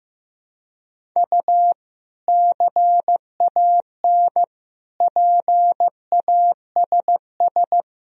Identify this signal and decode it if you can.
Morse code, heard as a keyed tone, about 16 wpm, 700 Hz. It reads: U CAN PASS
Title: Sentences from Top 400 Words and Abbreviations 15wpm